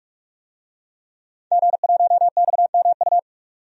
Morse code, heard as a keyed tone, about 44 wpm, 700 Hz. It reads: G1XMU
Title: DXCC-100 Call Signs - 5 Characters - 45wpm